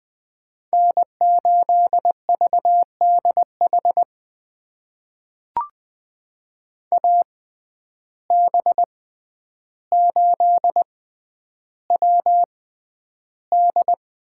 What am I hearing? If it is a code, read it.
Morse code, heard as a keyed tone, about 20 wpm, spaced as for 10 wpm, 700 Hz. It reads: N8VDH AB8WD
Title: USA Call Signs - 5 Characters - 6x Character Spacing - 20wpm